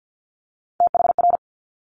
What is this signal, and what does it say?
Morse code, heard as a keyed tone, about 50 wpm, 690 Hz. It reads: THR
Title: Top Three Letter Combinations in Words 50wpm